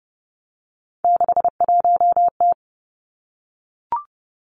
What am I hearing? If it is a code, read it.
Morse code, heard as a keyed tone, about 30 wpm, 700 Hz. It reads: 61T
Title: Three Challenging Letter-Number Combinations 30wpm